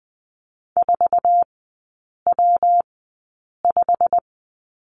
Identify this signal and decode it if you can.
Morse code, heard as a keyed tone, about 20 wpm, spaced as for 8 wpm, 700 Hz. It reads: 4W5
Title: 184 Lesson 30e - Letter b - Three Characters 20wpm